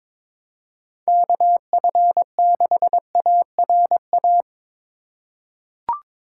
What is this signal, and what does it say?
Morse code, heard as a keyed tone, about 21 wpm, 700 Hz. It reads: KF6ARA
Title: USA Call Signs - 6 Characters - 2x Character Spacing - 22wpm